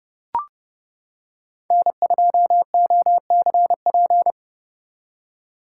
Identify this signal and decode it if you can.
Morse code, heard as a keyed tone, about 29 wpm, 700 Hz. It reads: N2OCP